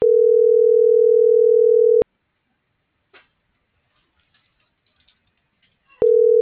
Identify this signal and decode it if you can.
Ringback tone (North America), 440 + 480 Hz.